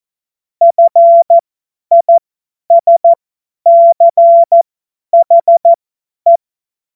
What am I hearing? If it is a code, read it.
Morse code, heard as a keyed tone, about 14 wpm, spaced as for 10 wpm, 680 Hz. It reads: FISCHE
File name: Morsekurs / Witze / Witz 15. Februar 2022 50 F70